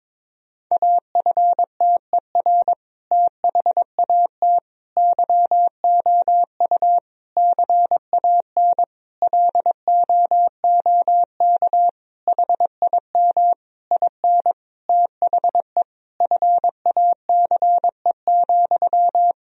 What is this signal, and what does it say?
Morse code, heard as a keyed tone, about 22 wpm, 700 Hz. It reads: AFTER THAT YOU CAN LOOK HIM IN THE FACE,
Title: Sentences from Top 1000 Words - Review Part 2 - 22wpm